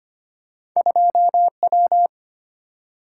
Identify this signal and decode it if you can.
Morse code, heard as a keyed tone, about 25 wpm, 690 Hz. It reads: 2W
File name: Two Letter-Number Combinations - 25wpm